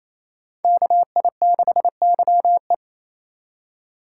Morse code, heard as a keyed tone, about 31 wpm, 700 Hz. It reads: KI6YE
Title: Speed Racing - USA 2x2 Call Signs 28wpm